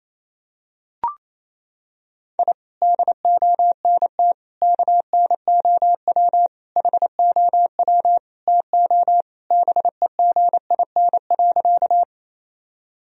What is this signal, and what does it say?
Morse code, heard as a keyed tone, about 28 wpm, 700 Hz. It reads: I DONT KNOW HOW TO BEGIN.
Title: Sentences from Top 1000 Words - Review Part 1 - 28wpm